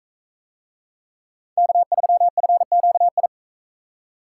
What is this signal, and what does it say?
Morse code, heard as a keyed tone, about 42 wpm, 700 Hz. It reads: K3FQI